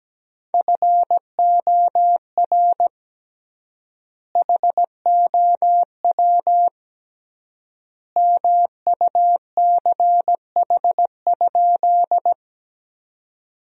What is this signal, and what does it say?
Morse code, heard as a keyed tone, about 17 wpm, 700 Hz. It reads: FOR HOW MUCH?
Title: 3x Word-Spacing - Sentences from Top 200 Words - 17wpm